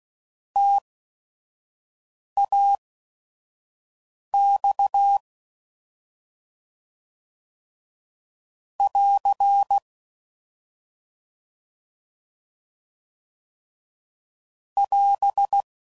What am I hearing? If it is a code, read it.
Morse code, heard as a keyed tone, about 16 wpm, spaced as for 5 wpm, 800 Hz. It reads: TAX <AR> <AS>